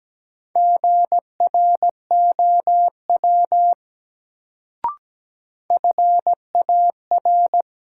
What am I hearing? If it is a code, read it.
Morse code, heard as a keyed tone, about 17 wpm, 700 Hz. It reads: GROW FAR
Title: Sentences from Top 300 Words - New Word Drill - 17wpm